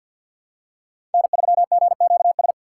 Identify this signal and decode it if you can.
Morse code, heard as a keyed tone, about 47 wpm, 700 Hz. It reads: N3GQS
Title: USA Call Signs - 5 Characters - 6x Character Spacing - 50wpm